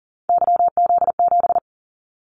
Morse code, heard as a keyed tone, about 38 wpm, 700 Hz. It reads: YZ7
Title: Callsign Prefixes for Top 25 DXCC - 40wpm